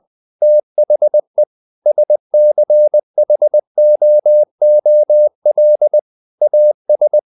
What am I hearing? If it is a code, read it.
Morse code, heard as a keyed tone, about 20 wpm, 600 Hz. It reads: THE SCHOOL AS
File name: the old man and the sea part 2 20wpm